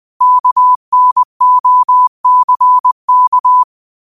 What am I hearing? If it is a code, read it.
Morse code, heard as a keyed tone, about 20 wpm, 1000 Hz. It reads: KNOCK
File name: qotdfiles / 20wpm